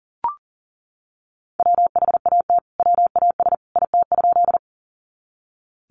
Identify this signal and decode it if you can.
Morse code, heard as a keyed tone, about 38 wpm, 700 Hz. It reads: WHAT WAS IT?
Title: Sentences from Top 100 Words 40wpm